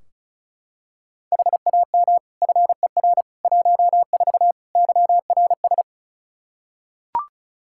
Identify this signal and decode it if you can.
Morse code, heard as a keyed tone, about 34 wpm, 700 Hz. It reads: HAM FER 14 YRS
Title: Practice QSO - Exchange 3 - More Info - 35wpm